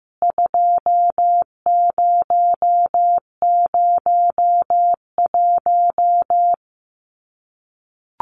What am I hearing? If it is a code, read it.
Morse code, heard as a keyed tone, about 15 wpm, 700 Hz. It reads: 2001
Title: CWT Exchange 15wpm